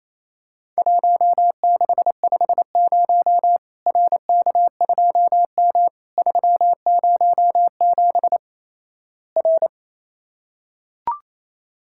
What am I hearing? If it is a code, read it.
Morse code, heard as a keyed tone, about 28 wpm, 700 Hz. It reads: 1650 RK2M 307 R
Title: Worked All Europe DX Contest - QTC Exchange - 28wpm